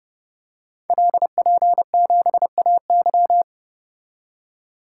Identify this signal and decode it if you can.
Morse code, heard as a keyed tone, about 30 wpm, 700 Hz. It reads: LP7AY
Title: Code Groups - 5 Letter:Numbers - 30wpm